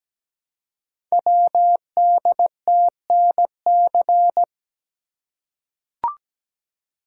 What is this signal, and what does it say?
Morse code, heard as a keyed tone, about 17 wpm, 700 Hz. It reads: WDTNC
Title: Code Groups - 5 Letters - 17wpm